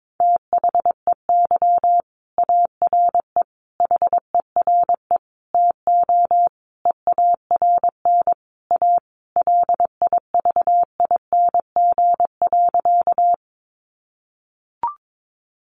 Morse code, heard as a keyed tone, about 22 wpm, 700 Hz. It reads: THEY ARE HERE TO EARN A LIVING.
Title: Sentences from Top 1500 Words 22wpm